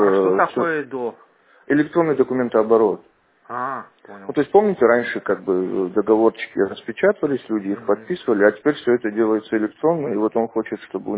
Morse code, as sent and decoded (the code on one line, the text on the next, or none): none